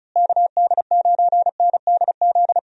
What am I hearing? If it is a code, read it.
Morse code, heard as a keyed tone, about 36 wpm, 700 Hz. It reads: KD9NDZ
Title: USA Call Signs - 6 Characters - Repeated 3x - 35wpm